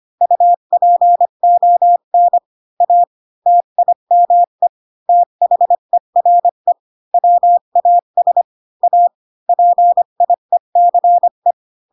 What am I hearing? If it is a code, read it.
Morse code, heard as a keyed tone, about 25 wpm, 700 Hz. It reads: UPON A TIME THERE WAS A PIECE
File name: Pinocchio / Pinocchio ch1 20 25 ik1hge